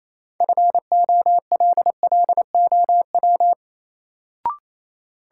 Morse code, heard as a keyed tone, about 28 wpm, 700 Hz. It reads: FOLLOW
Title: Sentences from Top 200 Words - New Word Drill - 28wpm